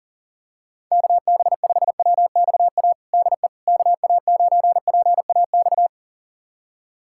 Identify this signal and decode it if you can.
Morse code, heard as a keyed tone, about 40 wpm, 700 Hz. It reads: KB5WXA DE KA9PAX
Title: USA Callsign DE Callsign - 6 Characters - 40wpm